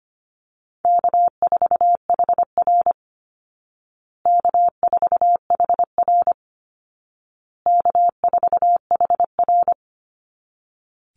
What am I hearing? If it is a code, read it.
Morse code, heard as a keyed tone, about 25 wpm, 700 Hz. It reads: K4HR K4HR K4HR
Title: Easy Call Signs Repeated 3x - 25wpm